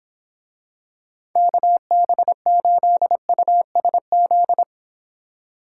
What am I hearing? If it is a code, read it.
Morse code, heard as a keyed tone, about 25 wpm, 700 Hz. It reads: KB8USZ